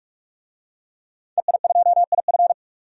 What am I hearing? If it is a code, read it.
Morse code, heard as a keyed tone, about 46 wpm, 700 Hz. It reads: EI2IF
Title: DXCC-100 Call Signs - 5 Characters - 45wpm